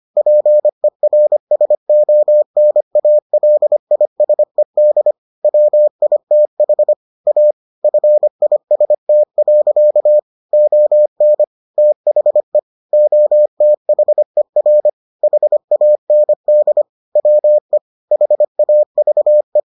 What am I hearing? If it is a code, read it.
Morse code, heard as a keyed tone, about 25 wpm, 600 Hz. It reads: PERSONALISED WITH A FIST. ON THE OTHER HAND WE HAVE